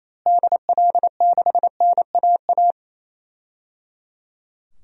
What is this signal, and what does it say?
Morse code, heard as a keyed tone, about 28 wpm, 700 Hz. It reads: DL6NAA
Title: DXCC-100 Call Signs - 6 Characters - Repeated 3x - 28wpm